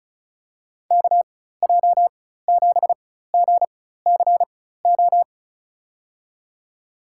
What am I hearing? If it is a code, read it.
Morse code, heard as a keyed tone, about 34 wpm, spaced as for 16 wpm, 700 Hz. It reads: KJ7GCO
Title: USA Call Signs - 6 Characters - 4x Character Spacing - 35wpm